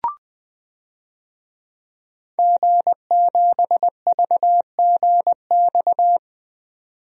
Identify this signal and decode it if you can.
Morse code, heard as a keyed tone, about 21 wpm, 700 Hz. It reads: G7VGX
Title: DXCC-100 Call Signs - 5 Characters - 20wpm